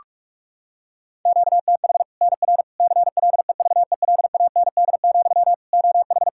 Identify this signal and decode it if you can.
Morse code, heard as a keyed tone, about 44 wpm, 700 Hz. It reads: QTH NR CLEVELAND, OH
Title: Practice QSO - Exchange 1 - Beginning - 45wpm